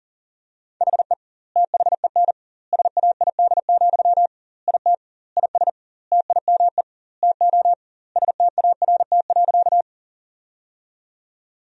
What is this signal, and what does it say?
Morse code, heard as a keyed tone, about 40 wpm, 700 Hz. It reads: HE THEN SAID, IT IS TIME TO START.
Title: 2x Word-Spacing - Sentences from Top 300 Words - 40wpm